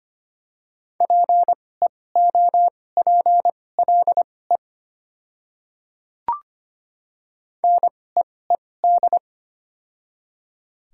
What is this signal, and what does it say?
Morse code, heard as a keyed tone, about 25 wpm, spaced as for 18 wpm, 700 Hz. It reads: PEOPLE NEED
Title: 2x Character Spacing - Top 200 Words - 25wpm